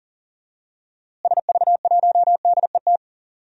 Morse code, heard as a keyed tone, about 40 wpm, 700 Hz. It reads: SV1DET
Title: DXCC-100 Call Signs - 6 Characters - Repeated 3x - 40wpm